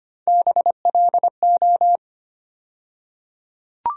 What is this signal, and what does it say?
Morse code, heard as a keyed tone, about 25 wpm, 700 Hz. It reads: BLO
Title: USA Call Sign Postfixes - 3 Letters 25wpm